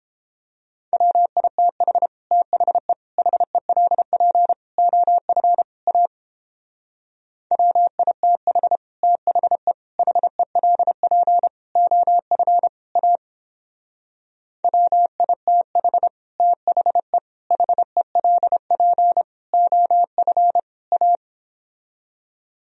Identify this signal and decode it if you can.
Morse code, heard as a keyed tone, about 29 wpm, 700 Hz. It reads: WITH THE HELP OF A WITH THE HELP OF A WITH THE HELP OF A